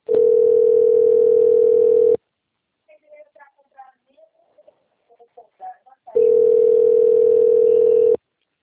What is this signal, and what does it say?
Ringback tone (North America), 440 + 480 Hz, with speech.
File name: rg-1600-unknown- - - .25522